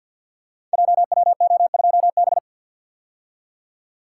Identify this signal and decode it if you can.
Morse code, heard as a keyed tone, about 49 wpm, 700 Hz. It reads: JWO2B
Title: Code Groups - 5 Letter:Numbers - 50wpm